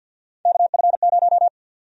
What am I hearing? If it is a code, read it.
Morse code, heard as a keyed tone, about 52 wpm, 700 Hz. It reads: KF0
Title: USA Call Sign Prefixes - 50wpm